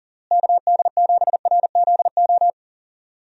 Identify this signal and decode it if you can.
Morse code, heard as a keyed tone, about 40 wpm, 700 Hz. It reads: KD7RZO
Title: USA Call Signs - 6 Characters - Repeated 3x - 40wpm